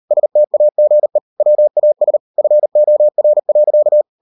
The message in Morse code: STAGE WAS FOR.

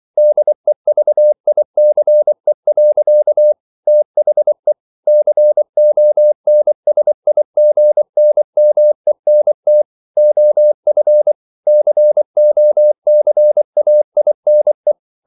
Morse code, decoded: DEVICE. THE CONSIGNMENT OF COCAINE

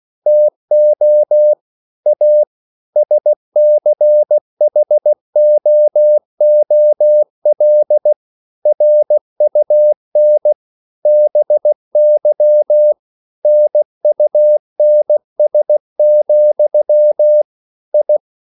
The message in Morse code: TO A SCHOOL RUN BY NUNS, I

